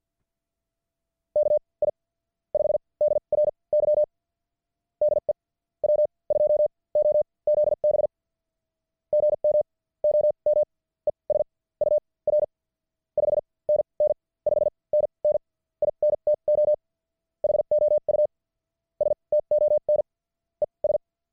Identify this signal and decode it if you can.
Morse code, heard as a keyed tone, about 50 wpm, spaced as for 28 wpm, 600 Hz. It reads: KI5DRY DE W2OZB GM OM ES UR 5NN 5NN INTO HOUSTON ES
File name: archived / CWBros